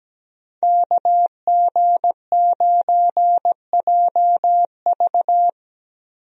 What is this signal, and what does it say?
Morse code, heard as a keyed tone, about 17 wpm, 700 Hz. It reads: KG9JV
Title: USA Call Signs - 5 Characters - 17wpm